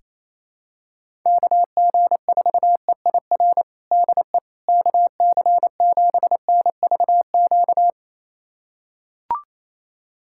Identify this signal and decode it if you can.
Morse code, heard as a keyed tone, about 28 wpm, 700 Hz. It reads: KG4EIR DE KC7NVQ
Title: USA Callsign DE Callsign 28wpm